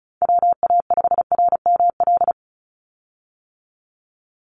Morse code, heard as a keyed tone, about 34 wpm, 700 Hz. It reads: WA5RML
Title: Call Signs - 35wpm